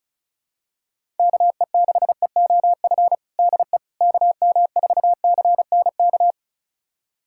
Morse code, heard as a keyed tone, about 36 wpm, 700 Hz. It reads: KE6EOF DE KM4CNK
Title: USA Callsign DE Callsign - 6 Characters - 35wpm